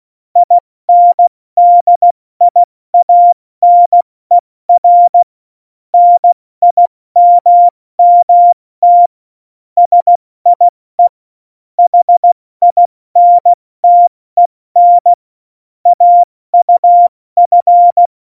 Morse code, heard as a keyed tone, about 16 wpm, 700 Hz. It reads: INDIANER NIMMT SIE HINTEN AUF
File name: Morsekurs / Witze / Witz 11 August 2020, 70 F80